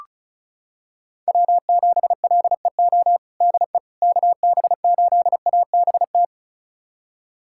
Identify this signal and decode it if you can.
Morse code, heard as a keyed tone, about 36 wpm, 700 Hz. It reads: W7LEO DE KB8ABT
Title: USA Callsign DE Callsign 35wpm